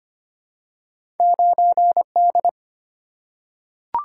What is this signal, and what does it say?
Morse code, heard as a keyed tone, about 25 wpm, 700 Hz. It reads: 9D
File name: Two Letter-Number Combinations - 25wpm